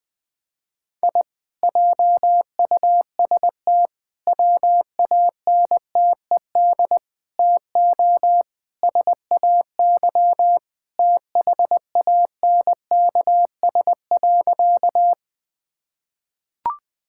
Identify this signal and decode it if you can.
Morse code, heard as a keyed tone, about 21 wpm, 700 Hz. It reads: I JUST WANTED TO SAY THANKS.
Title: Sentences with 4-Word Phrases - Spoken Beginning - 20wpm